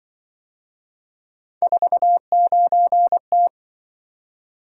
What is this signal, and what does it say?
Morse code, heard as a keyed tone, about 25 wpm, 700 Hz. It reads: 49T